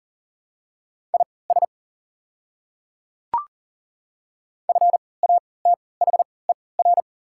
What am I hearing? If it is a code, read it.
Morse code, heard as a keyed tone, about 40 wpm, spaced as for 23 wpm, 700 Hz. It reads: IS FATHER
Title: 3x Character Spacing - Top 200 Words - 40wpm